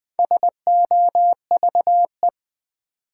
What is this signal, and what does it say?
Morse code, heard as a keyed tone, about 20 wpm, 700 Hz. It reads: SOVE
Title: Build A Word - 4 Letters At A Time - 11-Letter Words - 20wpm